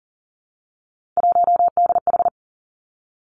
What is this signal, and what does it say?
Morse code, heard as a keyed tone, about 39 wpm, 700 Hz. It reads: 1DH
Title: Three Challenging Letter-Number Combinations 40wpm